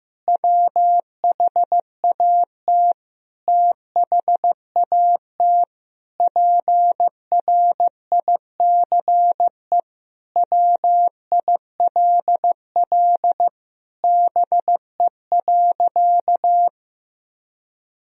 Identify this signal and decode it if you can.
Morse code, heard as a keyed tone, about 16 wpm, 700 Hz. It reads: WHAT THAT PRICE WILL BE.
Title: Sentences from Top 1000 Words - Review Part 4 - 15wpm